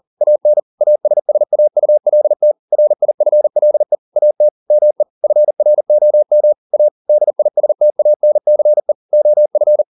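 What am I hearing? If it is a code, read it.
Morse code, heard as a keyed tone, about 40 wpm, 600 Hz. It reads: AN ASSAULT RIFLE AT ME FROM A DISTANCE OF
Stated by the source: Morse code at 40 wpm, practice text, CW med Gnister.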